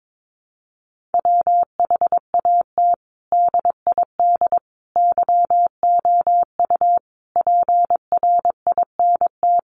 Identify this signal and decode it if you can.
Morse code, heard as a keyed tone, about 22 wpm, 700 Hz. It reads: WHAT DID YOU PRINT
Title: Sentences from Top 1400 Words 22wpm